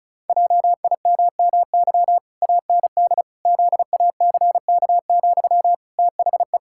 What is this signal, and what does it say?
Morse code, heard as a keyed tone, about 36 wpm, 700 Hz. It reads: JIMMY AND ZACK, THE
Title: Pangram Challenge v2 - 35wpm